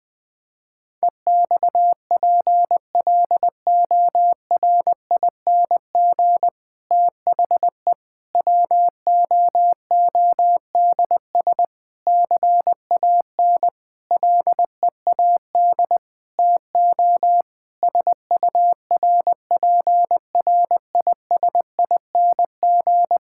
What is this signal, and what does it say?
Morse code, heard as a keyed tone, about 21 wpm, 700 Hz. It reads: EXPLORING THE WOODS CAN LEAD TO SURPRISING
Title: Difficult Sentences with Hints - 20wpm